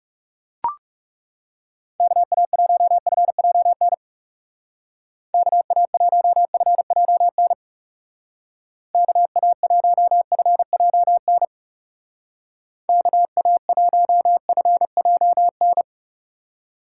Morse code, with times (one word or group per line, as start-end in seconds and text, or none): 2.0-3.9 KA1FJN
5.3-7.5 KA1FJN
8.9-11.5 KA1FJN
12.9-15.8 KA1FJN